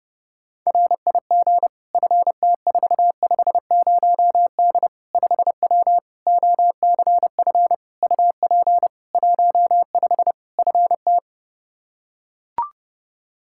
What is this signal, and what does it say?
Morse code, heard as a keyed tone, about 30 wpm, 700 Hz. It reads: RIG FT450D 5W OCF UP 15 FT
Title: K1USN SST QSO Practice 30wpm